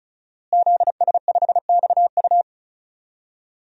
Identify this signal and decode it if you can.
Morse code, heard as a keyed tone, about 36 wpm, 700 Hz. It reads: ZS5XU
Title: DXCC-100 Call Signs - 5 Characters - Repeated 3x - 35wpm